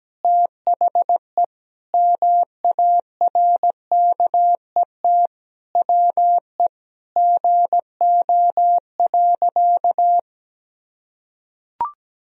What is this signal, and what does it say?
Morse code, heard as a keyed tone, about 17 wpm, 700 Hz. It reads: THE MARKET WE GO.
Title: Sentences from Top 1000 Words - Review Part 3 - 17wpm